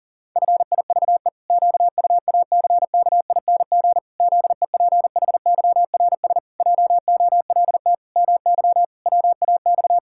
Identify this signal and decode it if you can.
Morse code, heard as a keyed tone, about 40 wpm, 700 Hz. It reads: FIVE QUACKING ZEPHYRS JOLT MY WAX
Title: Pangram Challenge v2 - 40wpm